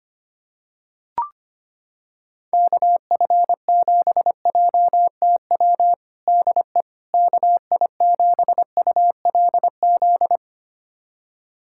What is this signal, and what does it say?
Morse code, heard as a keyed tone, about 25 wpm, 700 Hz. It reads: KF7JTW DE KI7ULZ
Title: USA Callsign DE Callsign 25wpm